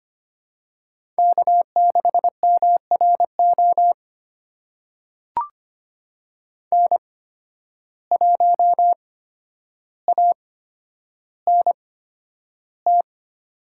Morse code, heard as a keyed tone, about 25 wpm, spaced as for 11 wpm, 700 Hz. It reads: K6MRO N1ANT
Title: USA Call Signs - 5 Characters - 8x Character Spacing - 25wpm